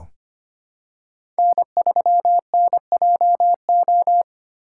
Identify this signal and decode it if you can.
Morse code, heard as a keyed tone, about 25 wpm, 700 Hz. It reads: N3NJO
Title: USA Call Signs - 5 Characters - 3x Character Spacing - 25wpm